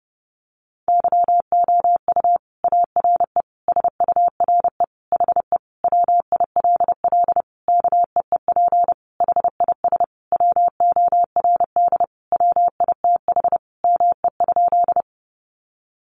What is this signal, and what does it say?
Morse code, heard as a keyed tone, about 30 wpm, 700 Hz. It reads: YOU ARE SURE HE WILL KEEP HIS WORD WITH ME?
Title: Sentences from Top 300 Words 30wpm